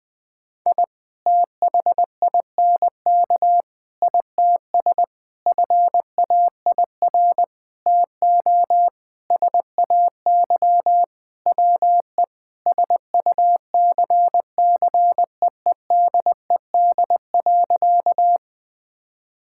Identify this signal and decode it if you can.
Morse code, heard as a keyed tone, about 20 wpm, 700 Hz. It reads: I THINK ITS FAIR TO SAY WE SUCCEEDED.
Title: Sentences with 6-Word Phrases - 20wpm